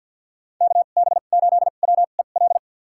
Morse code, heard as a keyed tone, about 52 wpm, spaced as for 36 wpm, 700 Hz. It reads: KB8WEL
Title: USA Call Signs - 2x Character Spacing - 50wpm